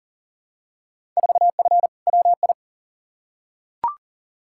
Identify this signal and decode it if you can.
Morse code, heard as a keyed tone, about 40 wpm, 700 Hz. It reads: 4F WI
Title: Field Day Exchanges - 40wpm